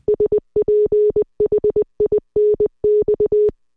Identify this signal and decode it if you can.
Morse code, heard as a keyed tone, about 20 wpm, 420 Hz. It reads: SPHINX